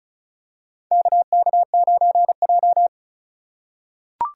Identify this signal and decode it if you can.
Morse code, heard as a keyed tone, about 34 wpm, 700 Hz. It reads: KK9J
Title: USA Call Signs - 4 Characters - 4x Character Spacing - 35wpm